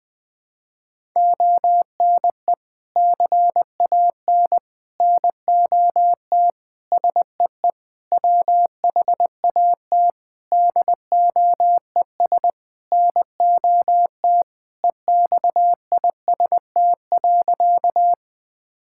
Morse code, heard as a keyed tone, about 20 wpm, 700 Hz. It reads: ONE CAN NOT SEE WHAT DOES NOT EXIST.
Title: Sentences from Top 1000 Words - Review Part 5 - 20wpm